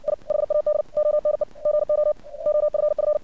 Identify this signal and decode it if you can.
Morse code, heard as a keyed tone, about 56 wpm, 600 Hz. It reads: I HAD ONE GO OFF
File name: archived / CRSnet